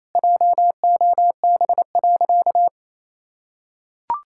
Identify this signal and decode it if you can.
Morse code, heard as a keyed tone, about 28 wpm, 700 Hz. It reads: JOB.
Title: Sentences from Top 1600 Words 28wpm